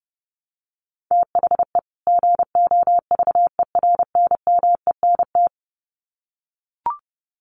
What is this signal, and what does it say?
Morse code, heard as a keyed tone, about 30 wpm, 700 Hz. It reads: THE GOVERNMENT
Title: Sets of 2 Words 30wpm